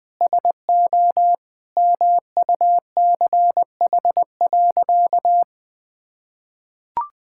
Morse code, heard as a keyed tone, about 20 wpm, 700 Hz. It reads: SO MUCH.
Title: Sentences with 2-Word Phrases - Spoken Beginning - 20wpm